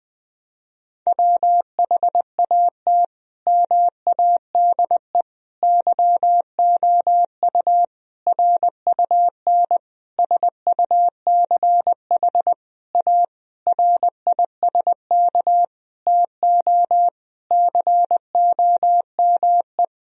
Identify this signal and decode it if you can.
Morse code, heard as a keyed tone, about 21 wpm, 700 Hz. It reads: WHAT MADE YOU RUN SUCH A RISK TO COME
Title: Sentences from Top 1000 Words - Review Part 4 - 20wpm